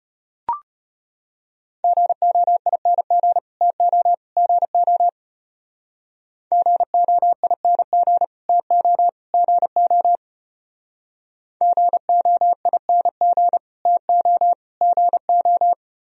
Morse code, heard as a keyed tone, about 32 wpm, 700 Hz. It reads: GOING TO GO GOING TO GO GOING TO GO